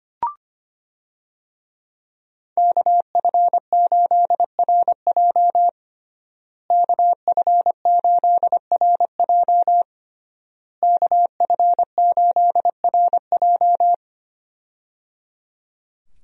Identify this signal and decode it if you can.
Morse code, heard as a keyed tone, about 25 wpm, 700 Hz. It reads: KF8RJ KF8RJ KF8RJ
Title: USA Call Signs - 5 Characters - Repeated 3x - 25wpm